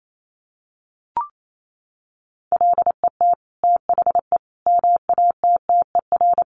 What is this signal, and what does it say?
Morse code, heard as a keyed tone, about 28 wpm, 700 Hz. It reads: LET THE MATTER
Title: Sentences from Top 1400 Words 28wpm